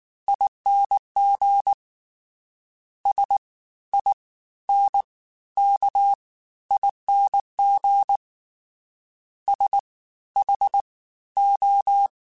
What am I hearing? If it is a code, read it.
Morse code, heard as a keyed tone, about 19 wpm, spaced as for 13 wpm, 800 Hz. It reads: ING SINKING SHO